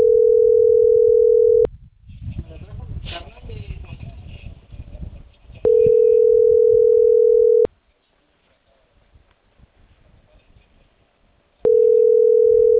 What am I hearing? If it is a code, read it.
Ringback tone (North America), 440 + 480 Hz.